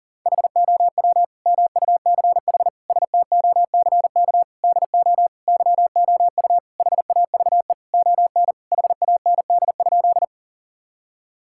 Morse code, heard as a keyed tone, about 37 wpm, 700 Hz. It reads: HOW MUCH STOCK DO YOU HAVE ON HAND?
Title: Sentences from Top 1000 Words - Review Part 5 - 40wpm